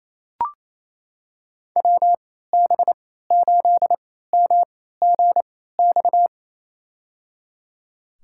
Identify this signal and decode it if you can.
Morse code, heard as a keyed tone, about 28 wpm, spaced as for 16 wpm, 700 Hz. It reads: WB8MGX
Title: USA Call Signs - 6 Characters - 3x Character Spacing - 28wpm